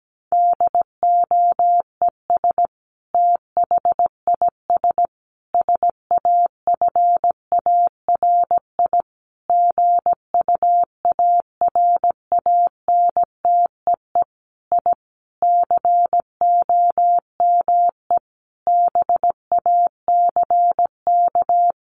Morse code, heard as a keyed tone, about 17 wpm, 700 Hz. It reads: DOES THIS SAFARI GUARANTEE I COME BACK
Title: A Sound Of Thunder 17wpm